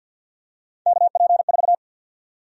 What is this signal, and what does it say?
Morse code, heard as a keyed tone, about 53 wpm, 700 Hz. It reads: KP4
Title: USA Call Sign Prefixes - 50wpm